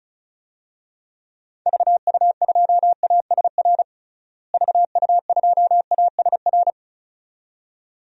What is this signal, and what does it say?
Morse code, heard as a keyed tone, about 36 wpm, 700 Hz. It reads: VU2ASR VU2ASR
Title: DXCC-100 Call Signs - 6 Characters - Repeated 2x - 35wpm